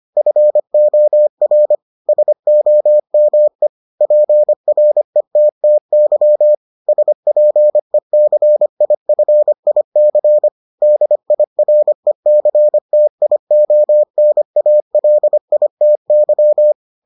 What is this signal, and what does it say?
Morse code, heard as a keyed tone, about 25 wpm, 600 Hz. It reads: FOR SOME PRETTY SPECIFIC DIRECTIONALITY